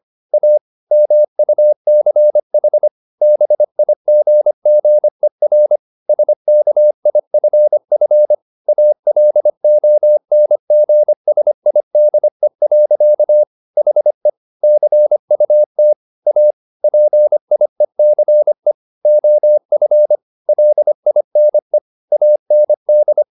Morse code, heard as a keyed tone, about 25 wpm, 600 Hz. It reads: A MUCH BIGGER SKIFF ALONGSIDE. HE CUT A PIECE OF LINE AND